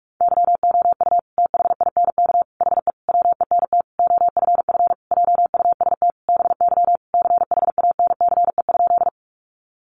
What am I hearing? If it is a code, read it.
Morse code, heard as a keyed tone, about 44 wpm, 700 Hz. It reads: YOU THINK HE WENT OFF JUST BY CHANCE?
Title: Sentences from Top 800 Words 45wpm